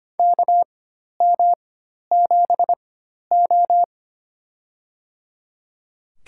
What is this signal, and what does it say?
Morse code, heard as a keyed tone, about 25 wpm, spaced as for 12 wpm, 700 Hz. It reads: KM7O
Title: USA Call Signs - 4 Characters - 4x Character Spacing - 25wpm